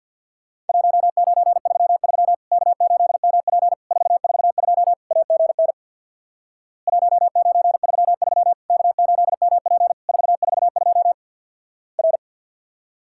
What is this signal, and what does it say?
Morse code, heard as a keyed tone, about 50 wpm, 700 Hz. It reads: 1933 K8MP 442 AGN 1933 K8MP 442 R
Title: Worked All Europe DX Contest - QTC Exchange with Repeats - 50wpm